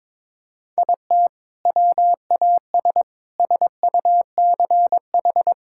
Morse code, heard as a keyed tone, about 22 wpm, 700 Hz. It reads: IT WAS SUCH
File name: practice / 22wpm / Sentences with 2-Word Phrases - 22wpm